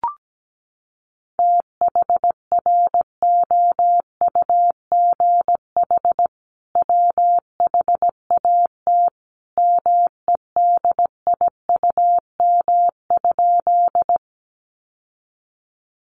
Morse code, heard as a keyed tone, about 17 wpm, 700 Hz. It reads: THROUGH WHAT MEDIUM?
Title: Sentences from Top 900 Words 17wpm